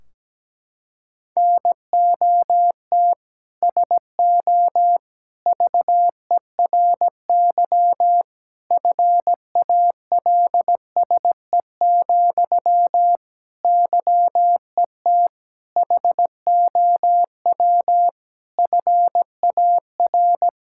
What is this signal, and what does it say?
Morse code, heard as a keyed tone, about 17 wpm, 700 Hz. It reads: NOT SO VERY FALSE, YET HOW FAR
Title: Sentences from Top 1600 Words 17wpm